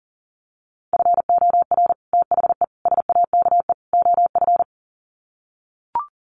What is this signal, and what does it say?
Morse code, heard as a keyed tone, about 40 wpm, 700 Hz. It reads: FOR THE SAKE OF
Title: Sets of 4 Words 40wpm